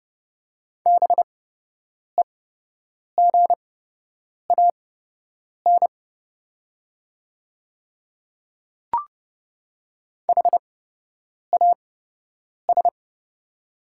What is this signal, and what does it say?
Morse code, heard as a keyed tone, about 29 wpm, spaced as for 8 wpm, 690 Hz. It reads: BEGAN HAS
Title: 8x Character Spacing - Top 500 Words - 30wpm